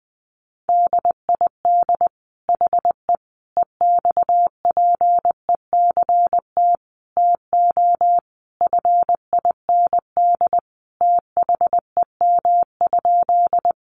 Morse code, heard as a keyed tone, about 20 wpm, 700 Hz. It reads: DID HE EXPECT TO FIND THEM?
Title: Sentences from Top 1100 Words 20wpm